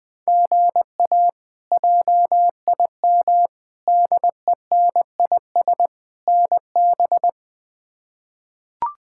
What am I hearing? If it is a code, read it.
Morse code, heard as a keyed tone, about 20 wpm, 700 Hz. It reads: GA JIM DENIS NB
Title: K1USN SST QSO Practice - 20wpm